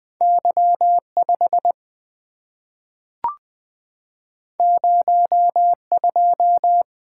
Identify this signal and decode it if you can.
Morse code, heard as a keyed tone, about 20 wpm, 700 Hz. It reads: Y 5 0 2
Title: Two Letter-Number Combinations - 20wpm